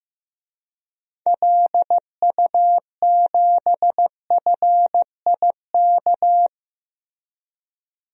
Morse code, heard as a keyed tone, about 16 wpm, 700 Hz. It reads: LU7FIK
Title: DXCC-100 Call Signs - 6 Characters - Repeated 2x - 15wpm